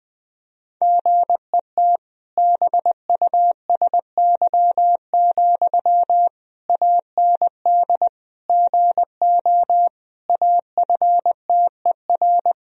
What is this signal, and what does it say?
Morse code, heard as a keyed tone, about 19 wpm, 700 Hz. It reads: GET BUSY, AND GO AFTER
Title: Sentences from Top 1000 Words - Review Part 3 - 20wpm